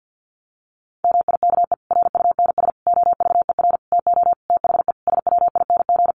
Morse code, heard as a keyed tone, about 50 wpm, 700 Hz. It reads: MIKE RUNS OVER TO THE SWING
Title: Additional Easy Phrases 50wpm